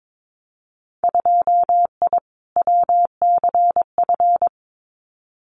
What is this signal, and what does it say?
Morse code, heard as a keyed tone, about 22 wpm, 700 Hz. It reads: 2I WCF
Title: Winter Field Day Exchanges 22wpm